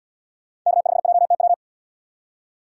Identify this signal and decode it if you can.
Morse code, heard as a keyed tone, about 68 wpm, 700 Hz. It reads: K5JTW